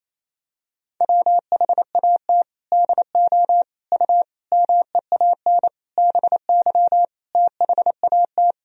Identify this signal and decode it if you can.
Morse code, heard as a keyed tone, about 28 wpm, 700 Hz. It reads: WHAT DO U MEAN BY THAT
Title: Sentences from Top 400 Words and Abbreviations 28wpm